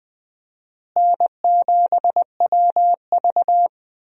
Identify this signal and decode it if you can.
Morse code, heard as a keyed tone, about 21 wpm, 700 Hz. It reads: N7WV
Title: USA Call Signs - 4 Characters - Repeated 2x - 20wpm